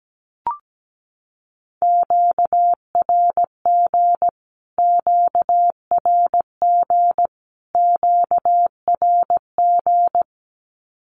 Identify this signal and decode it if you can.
Morse code, heard as a keyed tone, about 17 wpm, 700 Hz. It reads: QRG QRG QRG
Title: Q-Codes Repeated 3x 17wpm